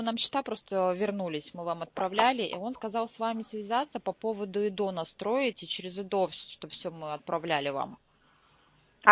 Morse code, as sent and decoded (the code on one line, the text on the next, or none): none